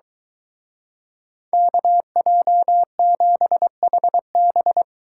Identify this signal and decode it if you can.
Morse code, heard as a keyed tone, about 23 wpm, 700 Hz. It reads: KJ7HB